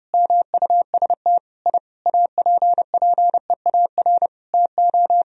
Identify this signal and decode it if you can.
Morse code, heard as a keyed tone, about 30 wpm, 700 Hz. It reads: MUST I APPEAR TO
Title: Sentences from Top 1000 Words - Review Part 2 - 30wpm